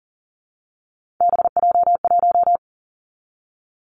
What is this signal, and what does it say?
Morse code, heard as a keyed tone, about 42 wpm, 700 Hz. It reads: BJ1
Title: Three Challenging Letter-Number Combinations 40wpm